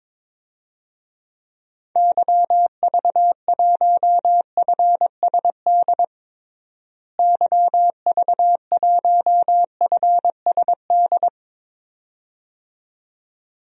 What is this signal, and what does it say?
Morse code, heard as a keyed tone, about 22 wpm, 700 Hz. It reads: YV1FSD YV1FSD
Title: DXCC-100 Call Signs - 6 Characters - Repeated 2x - 22wpm